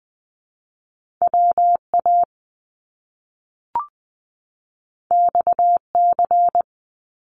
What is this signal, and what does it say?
Morse code, heard as a keyed tone, about 20 wpm, 700 Hz. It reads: WA XC
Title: Top Two Letter Combinations in Words 20wpm